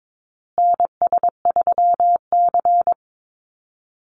Morse code, heard as a keyed tone, about 22 wpm, 700 Hz. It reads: NS3C
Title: Call Signs Repeated 2x - 22wpm